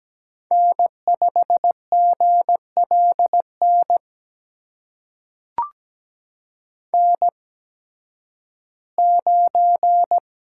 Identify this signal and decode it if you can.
Morse code, heard as a keyed tone, about 17 wpm, spaced as for 11 wpm, 700 Hz. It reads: N5GLN N9
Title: USA Call Signs - 8x Character Spacing - 17wpm